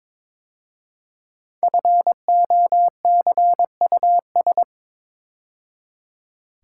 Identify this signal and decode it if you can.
Morse code, heard as a keyed tone, about 23 wpm, 700 Hz. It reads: FOCUS
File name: 5 Letter Words - First 3 Letters Given - 22wpm